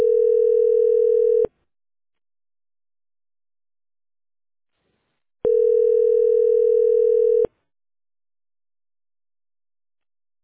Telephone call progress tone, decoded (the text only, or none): ringback tone